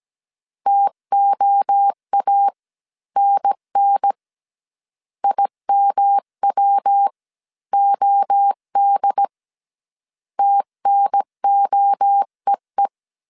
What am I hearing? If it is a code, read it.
Morse code, heard as a keyed tone, about 17 wpm, spaced as for 14 wpm, 800 Hz. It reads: TOANN IMWOD TNOEE